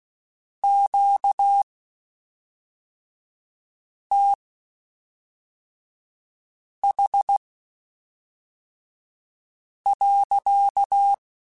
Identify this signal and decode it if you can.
Morse code, heard as a keyed tone, about 16 wpm, spaced as for 3 wpm, 800 Hz. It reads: QTH.
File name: Q Calls / Q Calls-mp3s / Q Calls sppc @5